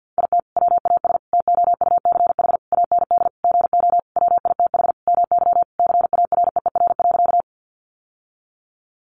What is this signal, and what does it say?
Morse code, heard as a keyed tone, about 50 wpm, 700 Hz. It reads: IT WAS TOUCH AND GO WITH MY CAREER.
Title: Sentences from Top 1200 Words 50wpm